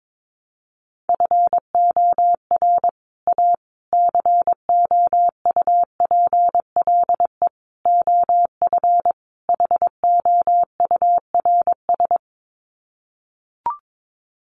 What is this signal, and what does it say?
Morse code, heard as a keyed tone, about 22 wpm, 700 Hz. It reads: FOR A COUPLE OF HOURS
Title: Sets of 5 Words Encore2 22wpm